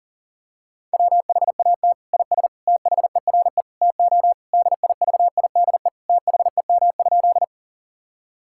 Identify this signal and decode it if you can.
Morse code, heard as a keyed tone, about 40 wpm, 700 Hz. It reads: WHAT IS THERE TO DIVIDE THEM?
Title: Sentences from Top 1600 Words 40wpm